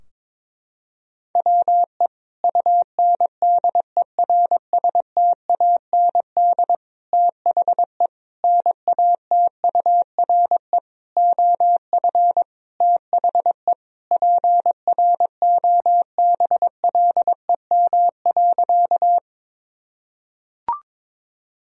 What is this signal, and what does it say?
Morse code, heard as a keyed tone, about 22 wpm, 700 Hz. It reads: WE UNDERSTAND THE NATURE OF THE PROBLEM.
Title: Sentences from Top 1000 Words - Review Part 2 - 22wpm